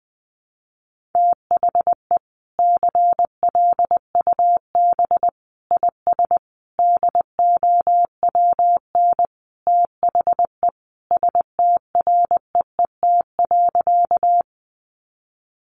Morse code, heard as a keyed tone, about 20 wpm, 700 Hz. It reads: THE CLUB IS DOWN THE STREET.
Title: Sentences from Top 1200 Words 20wpm